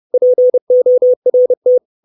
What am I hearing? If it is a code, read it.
Morse code, heard as a keyed tone, about 30 wpm, 500 Hz. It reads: PORT
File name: 632 words / 632 Words 30WPM